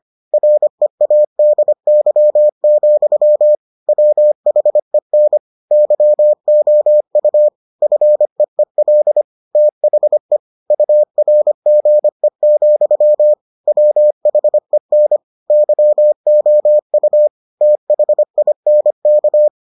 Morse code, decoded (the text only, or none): READY, WHEN YOU FEEL THE URGE, WHEN YOU THINK